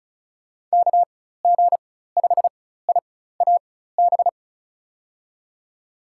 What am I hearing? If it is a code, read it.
Morse code, heard as a keyed tone, about 36 wpm, spaced as for 16 wpm, 700 Hz. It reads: KG5IAB
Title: USA Call Signs - 4x Character Spacing - 35wpm